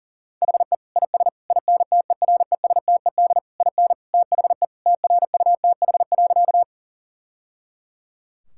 Morse code, heard as a keyed tone, about 40 wpm, 700 Hz. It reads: HE IS INTERESTED IN THE TRUTH.
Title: Sentences from Top 1000 Words - Review Part 6 - 40wpm